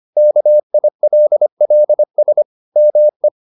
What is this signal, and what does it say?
Morse code, heard as a keyed tone, about 25 wpm, 600 Hz. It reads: KILLS ME